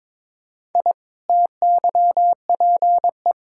Morse code, heard as a keyed tone, about 21 wpm, 700 Hz. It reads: I TYPE
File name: practice / 22wpm / Easy Sentences with Hints - 22wpm